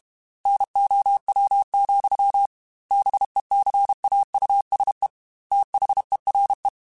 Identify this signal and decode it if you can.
Morse code, heard as a keyed tone, about 32 wpm, 800 Hz. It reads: NOW, BECAUSE THERE
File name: Technical Subjects / Technical Subjects-mp3s / Gates Intro @30